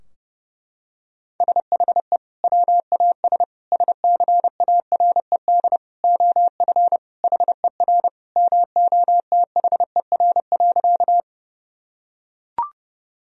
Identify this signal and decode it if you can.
Morse code, heard as a keyed tone, about 32 wpm, 700 Hz. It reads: SHE WAS SCARED OF HER MOTHER.
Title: Sentences from Top 1000 Words - Review Part 4 - 30wpm